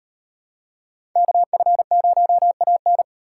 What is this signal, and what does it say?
Morse code, heard as a keyed tone, about 37 wpm, 700 Hz. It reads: KF0AN